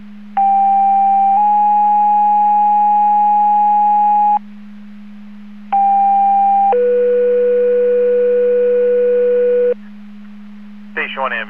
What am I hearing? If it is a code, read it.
Two-tone page, heard as a one-second tone, then a three-second tone, about 790 then 830 Hz, about 790 then 490 Hz. It is heard from 0.4 to 9.7 s.